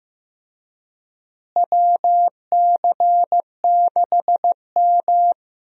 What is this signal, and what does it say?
Morse code, heard as a keyed tone, about 16 wpm, 700 Hz. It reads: WC6M
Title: USA Call Signs - 4 Characters - Repeated 2x - 15wpm